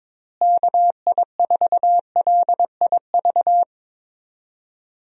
Morse code, heard as a keyed tone, about 22 wpm, 700 Hz. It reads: KI4LIV
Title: USA Call Signs - 6 Characters - Repeated 3x - 22wpm